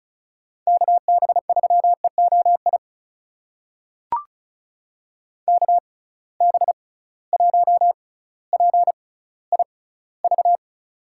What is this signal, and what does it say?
Morse code, heard as a keyed tone, about 36 wpm, spaced as for 18 wpm, 700 Hz. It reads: KB3EOI KB1PIV
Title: USA Call Signs - 6 Characters - 6x Character Spacing - 35wpm